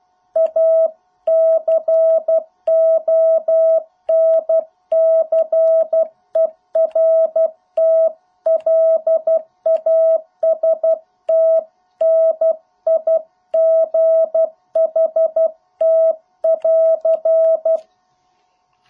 Morse code, heard as a keyed tone, about 12 wpm, 650 Hz. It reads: A CONCERT LAST NIGHT <AR>